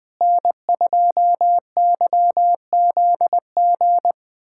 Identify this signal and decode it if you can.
Morse code, heard as a keyed tone, about 21 wpm, 700 Hz. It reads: N2YZG
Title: USA Callsign DE Callsign - 5 Characters - 20wpm